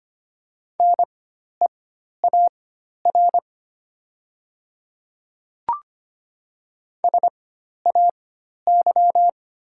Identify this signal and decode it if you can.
Morse code, heard as a keyed tone, about 25 wpm, spaced as for 12 wpm, 700 Hz. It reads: NEAR SAY
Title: 4x Character Spacing - Top 200 Words - 25wpm